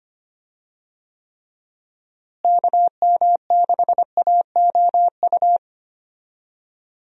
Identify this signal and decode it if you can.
Morse code, heard as a keyed tone, about 25 wpm, 700 Hz. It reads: KM6AOU
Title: USA Call Signs - 6 Characters - Repeated 3x - 25wpm